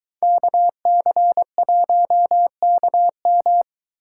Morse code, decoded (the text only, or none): KC1KM